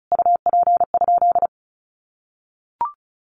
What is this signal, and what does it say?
Morse code, heard as a keyed tone, about 34 wpm, 700 Hz. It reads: UP?
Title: Sentences from Top 1300 Words 35wpm